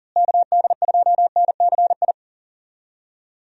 Morse code, heard as a keyed tone, about 40 wpm, 700 Hz. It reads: KD2NCI
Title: USA Call Signs - 6 Characters - Repeated 2x - 40wpm